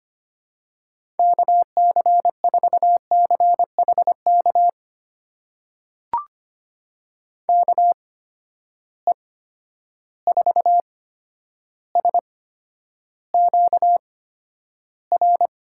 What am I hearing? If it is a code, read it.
Morse code, heard as a keyed tone, about 25 wpm, spaced as for 11 wpm, 700 Hz. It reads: KC4CHK KE4SQR
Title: USA Call Signs - 6 Characters - 8x Character Spacing - 25wpm